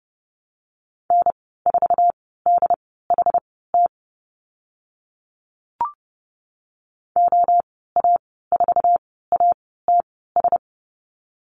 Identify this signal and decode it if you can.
Morse code, heard as a keyed tone, about 30 wpm, spaced as for 17 wpm, 700 Hz. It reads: N4DHT OA4ATS
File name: Lesson 14g - Number 2 - Call Signs 30wpm